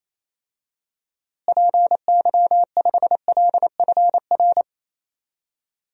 Morse code, heard as a keyed tone, about 28 wpm, 700 Hz. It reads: PY5LFR
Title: DXCC-100 Call Signs - 6 Characters - Repeated 2x - 28wpm